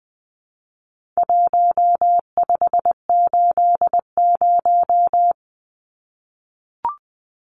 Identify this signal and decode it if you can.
Morse code, heard as a keyed tone, about 20 wpm, 700 Hz. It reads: 1580
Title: Numbers - Contest Style - 4-Digit 20wpm